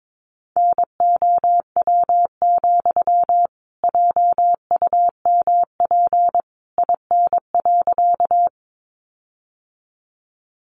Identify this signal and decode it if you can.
Morse code, heard as a keyed tone, about 22 wpm, 700 Hz. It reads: NOW, JUMP IN.
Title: Sentences from Top 1300 Words 22wpm